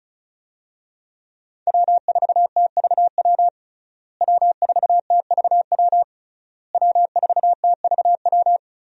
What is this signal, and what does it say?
Morse code, heard as a keyed tone, about 36 wpm, 700 Hz. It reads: W4TVW W4TVW W4TVW
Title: USA Call Signs - 5 Characters - Repeated 3x - 35wpm